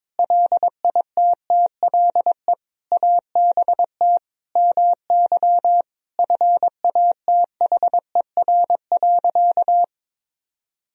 Morse code, heard as a keyed tone, about 22 wpm, 700 Hz. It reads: LITTLE ABT MY FATHER.
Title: Sentences from Top 400 Words and Abbreviations 22wpm